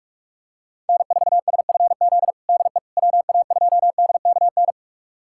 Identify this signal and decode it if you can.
Morse code, heard as a keyed tone, about 46 wpm, 690 Hz. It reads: N4SFZ DE WA2DKN
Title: USA Callsign DE Callsign 45wpm